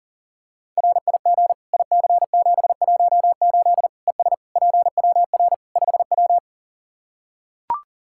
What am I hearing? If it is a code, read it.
Morse code, heard as a keyed tone, about 40 wpm, 700 Hz. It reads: RIG IC718 ES PWR 5W
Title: K1USN SST QSO Practice 40wpm